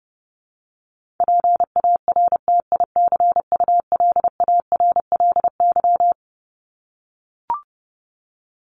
Morse code, heard as a keyed tone, about 30 wpm, 700 Hz. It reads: PARTICULARLY
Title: LY Suffix - Top 100 Words with Base Words 30wpm